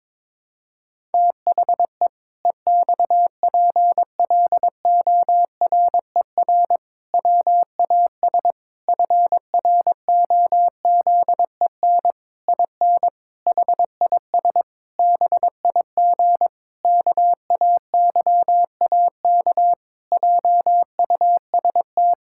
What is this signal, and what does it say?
Morse code, heard as a keyed tone, about 22 wpm, 700 Hz. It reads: THE EXPLORER WAS FROZEN IN HIS BIG KAYAK JUST
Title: Pangram Challenge v2 - 22wpm